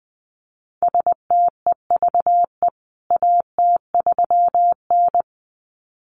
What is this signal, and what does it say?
Morse code, heard as a keyed tone, about 20 wpm, 700 Hz. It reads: STEVE AT3N
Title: CWT Exchange 20wpm